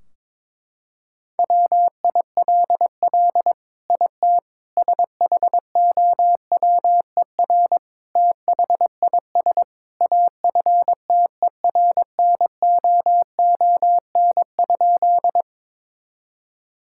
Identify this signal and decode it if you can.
Morse code, heard as a keyed tone, about 22 wpm, 700 Hz. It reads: WILL IT SHOWER THIS AFTERNOON?
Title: Sentences from Top 1600 Words 22wpm